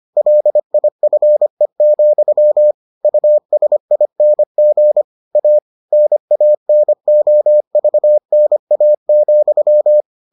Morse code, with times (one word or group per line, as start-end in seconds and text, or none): 0.2-2.7 LIFE,
3.0-5.0 USING
5.4-5.6 A
5.9-10.0 NANOVNA,